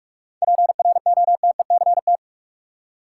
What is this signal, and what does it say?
Morse code, heard as a keyed tone, about 46 wpm, 700 Hz. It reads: PROTECT
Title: Seven Letter Words - 45wpm